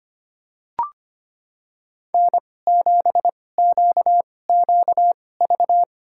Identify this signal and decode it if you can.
Morse code, heard as a keyed tone, about 25 wpm, spaced as for 18 wpm, 700 Hz. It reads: N7QQV
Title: USA Call Signs - 5 Characters - 2x Character Spacing - 25wpm